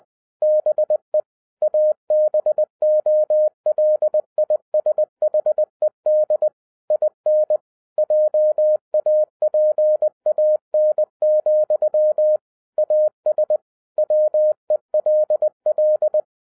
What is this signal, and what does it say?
Morse code, heard as a keyed tone, about 20 wpm, 610 Hz. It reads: BE ABOLISHED IN JAPAN, AS WELL